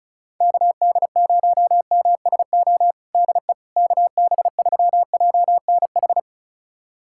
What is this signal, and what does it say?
Morse code, heard as a keyed tone, about 36 wpm, 700 Hz. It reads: KD0MSO DE KB3JNH
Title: USA Callsign DE Callsign - 6 Characters - 35wpm